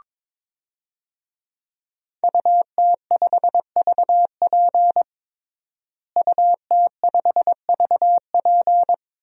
Morse code, heard as a keyed tone, about 22 wpm, 700 Hz. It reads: UT5VP UT5VP
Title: DXCC-100 Call Signs - 5 Characters - Repeated 3x - 22wpm